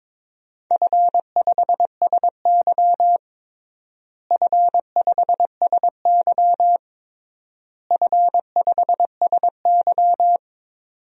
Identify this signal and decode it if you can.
Morse code, heard as a keyed tone, about 22 wpm, 700 Hz. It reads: F5SY F5SY F5SY
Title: DXCC-100 Call Signs - 4 Characters - Repeated 3x - 22wpm